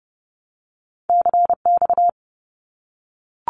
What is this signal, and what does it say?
Morse code, heard as a keyed tone, about 30 wpm, 700 Hz. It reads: CX
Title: Abbreviations Repeated 3x - 30wpm